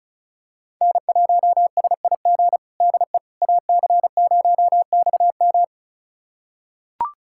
Morse code, heard as a keyed tone, about 36 wpm, 700 Hz. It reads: N1SIG DE AC0XM
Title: USA Callsign DE Callsign - 5 Characters - 35wpm